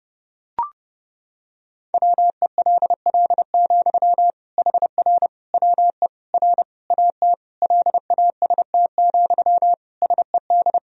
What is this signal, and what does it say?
Morse code, heard as a keyed tone, about 30 wpm, 700 Hz. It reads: WELL, HR WE R AT LAST, SED
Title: Sentences from Top 300 Words and Abbreviations 30wpm